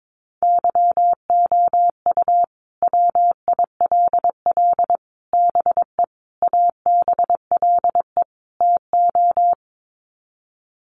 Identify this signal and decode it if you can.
Morse code, heard as a keyed tone, about 22 wpm, 700 Hz. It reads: YOU WILL BE ABLE TO
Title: Sets of 5 Words Encore2 22wpm